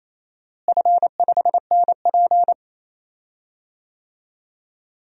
Morse code, heard as a keyed tone, about 28 wpm, 700 Hz. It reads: F5NP
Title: DXCC-100 Call Signs - 4 Characters - Repeated 3x - 28wpm